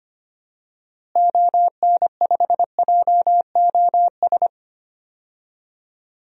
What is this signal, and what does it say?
Morse code, heard as a keyed tone, about 25 wpm, 700 Hz. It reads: ON5JOS
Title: DXCC-100 Call Signs - 6 Characters - Repeated 2x - 25wpm